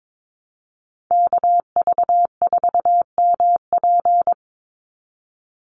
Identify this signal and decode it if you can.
Morse code, heard as a keyed tone, about 22 wpm, 700 Hz. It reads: KV4MP
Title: USA Call Signs 22wpm